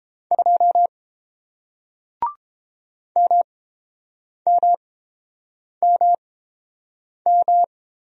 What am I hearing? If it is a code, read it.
Morse code, heard as a keyed tone, about 30 wpm, spaced as for 7 wpm, 700 Hz. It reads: 2 MMMM